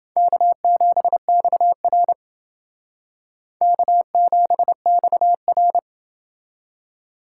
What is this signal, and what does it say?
Morse code, heard as a keyed tone, about 29 wpm, 700 Hz. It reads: K7XR K7XR